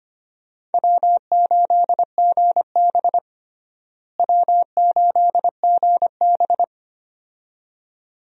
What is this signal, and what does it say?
Morse code, heard as a keyed tone, about 25 wpm, 700 Hz. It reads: W8GB W8GB
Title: USA Call Signs - 4 Characters - Repeated 2x - 25wpm